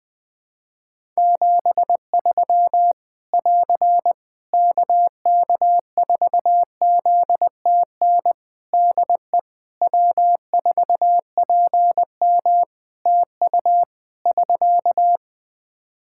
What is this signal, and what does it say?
Morse code, heard as a keyed tone, about 20 wpm, 700 Hz. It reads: 73 <AR> KK4ZTN DE W4PM TU <SK>
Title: Practice QSO - Exchange 4 - Ending - 20wpm